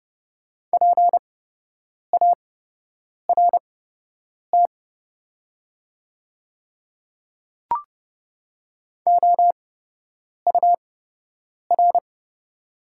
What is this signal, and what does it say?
Morse code, heard as a keyed tone, about 30 wpm, spaced as for 8 wpm, 700 Hz. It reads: PART OUR
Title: 8x Character Spacing - Top 200 Words - 30wpm